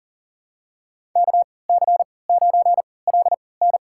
Morse code, heard as a keyed tone, about 40 wpm, spaced as for 23 wpm, 700 Hz. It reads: KC9LN
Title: USA Call Signs - 3x Character Spacing - 40wpm